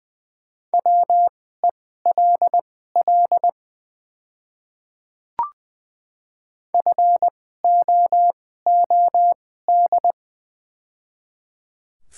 Morse code, heard as a keyed tone, about 20 wpm, spaced as for 14 wpm, 700 Hz. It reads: WELL FOOD
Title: 2x Character Spacing - Top 300 Words - 20wpm